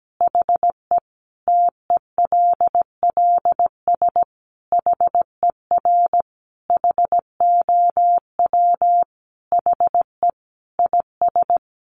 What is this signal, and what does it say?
Morse code, heard as a keyed tone, about 17 wpm, 700 Hz. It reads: HE TELLS HER HOW HE IS
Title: Additional Easy Phrases 17wpm